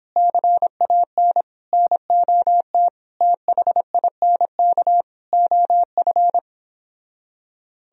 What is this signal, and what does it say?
Morse code, heard as a keyed tone, about 29 wpm, 700 Hz. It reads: CAN NOT THINK OF